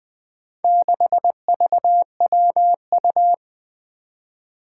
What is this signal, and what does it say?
Morse code, heard as a keyed tone, about 20 wpm, 700 Hz. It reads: 6VWU
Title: Four Challenging Letter-Number Combinations 20wpm